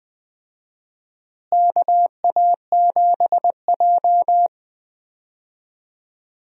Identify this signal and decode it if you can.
Morse code, heard as a keyed tone, about 21 wpm, 700 Hz. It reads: KA7J
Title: USA Call Signs - 4 Characters - Repeated 2x - 20wpm